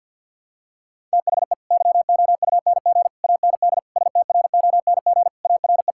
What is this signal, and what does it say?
Morse code, heard as a keyed tone, about 51 wpm, 700 Hz. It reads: THE YOUNG AND STRONG ARE
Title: Sentences from Top 1000 Words - Review Part 5 - 50wpm